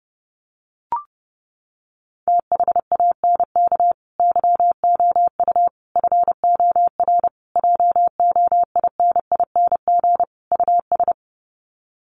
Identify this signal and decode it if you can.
Morse code, heard as a keyed tone, about 28 wpm, 700 Hz. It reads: THANK YOU FOR JOINING US
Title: Sets of 5 Words 30wpm